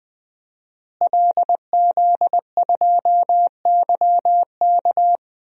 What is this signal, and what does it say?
Morse code, heard as a keyed tone, about 20 wpm, 700 Hz. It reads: LZ2YK
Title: DXCC-100 Call Signs - 5 Characters - Repeated 2x - 20wpm